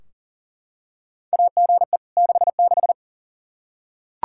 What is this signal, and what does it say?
Morse code, heard as a keyed tone, about 38 wpm, 700 Hz. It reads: AGE 66
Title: Practice QSO - Exchange 3 - More Info - 40wpm